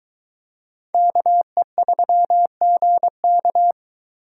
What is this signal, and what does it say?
Morse code, heard as a keyed tone, about 22 wpm, 700 Hz. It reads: KE3GK